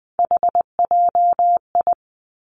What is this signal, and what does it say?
Morse code, heard as a keyed tone, about 20 wpm, 700 Hz. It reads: HJI
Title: Three Challenging Letter-Number Combinations 20wpm